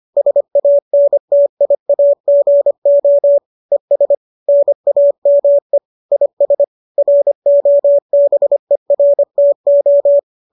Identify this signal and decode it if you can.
Morse code, heard as a keyed tone, about 25 wpm, 570 Hz. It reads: SANTIAGO ES NAME IS ROBERTO